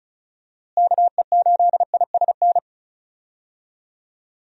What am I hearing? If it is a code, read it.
Morse code, heard as a keyed tone, about 36 wpm, 700 Hz. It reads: KE8ISN
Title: USA Call Signs - 6 Characters - Repeated 2x - 35wpm